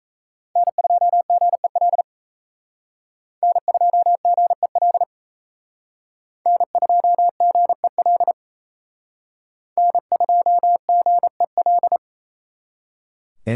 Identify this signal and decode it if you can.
Morse code, heard as a keyed tone, about 35 wpm, 700 Hz. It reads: N2GEL N2GEL N2GEL N2GEL